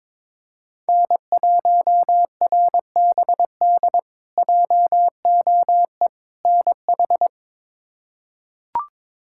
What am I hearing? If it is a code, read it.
Morse code, heard as a keyed tone, about 23 wpm, 700 Hz. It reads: N1RBD JOE NH
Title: K1USN SST QSO Practice - 22wpm